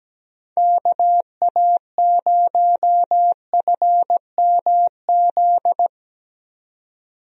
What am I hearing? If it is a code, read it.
Morse code, heard as a keyed tone, about 17 wpm, 700 Hz. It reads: KA0FMZ
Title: USA Call Signs - 6 Characters - 17wpm